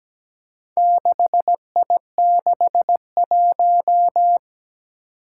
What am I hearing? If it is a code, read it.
Morse code, heard as a keyed tone, about 17 wpm, 700 Hz. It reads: 6I61
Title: Four Challenging Letter-Number Combinations 17wpm